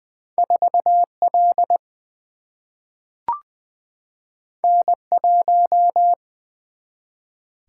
Morse code, heard as a keyed tone, about 20 wpm, 700 Hz. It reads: 4 L N 1
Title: Single-Letter Single-Number 20wpm